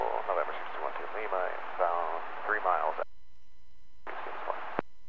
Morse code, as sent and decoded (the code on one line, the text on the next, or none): none